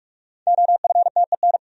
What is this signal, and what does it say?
Morse code, heard as a keyed tone, about 60 wpm, 700 Hz. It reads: OFTEN